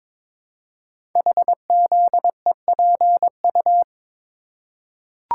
Morse code, heard as a keyed tone, about 22 wpm, 700 Hz. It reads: HZEPU
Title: Code Groups - 5 Letters - 22wpm